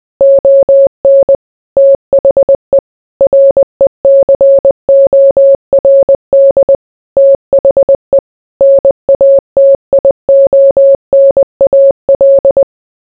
Morse code, heard as a keyed tone, about 20 wpm, 550 Hz. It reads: ON THE RECORD THE NATIONAL